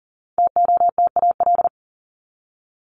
Morse code, heard as a keyed tone, about 40 wpm, 700 Hz. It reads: TOTAL
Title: Top 1251-1500 Words 40wpm